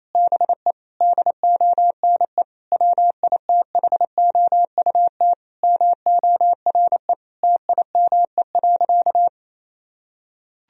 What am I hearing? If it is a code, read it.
Morse code, heard as a keyed tone, about 27 wpm, 700 Hz. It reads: BE DONE WITHOUT MORE TIME.
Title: Sentences from Top 1000 Words - Review Part 3 - 28wpm